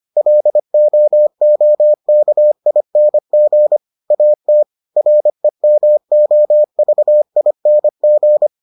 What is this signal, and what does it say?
Morse code, heard as a keyed tone, about 25 wpm, 600 Hz. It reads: LOOKING AT REMOVING